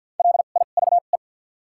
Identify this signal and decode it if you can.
Morse code, heard as a keyed tone, about 53 wpm, spaced as for 37 wpm, 700 Hz. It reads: LIVE
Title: 2x Character Spacing - Top 200 Words - 50wpm